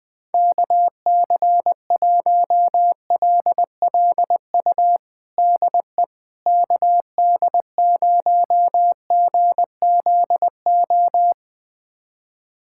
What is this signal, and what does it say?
Morse code, heard as a keyed tone, about 20 wpm, 700 Hz. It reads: KC1LLU DE KD0GZO
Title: USA Callsign DE Callsign 20wpm